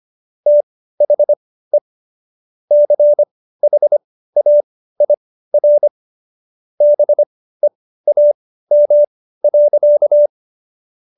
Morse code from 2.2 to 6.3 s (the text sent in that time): CHAIR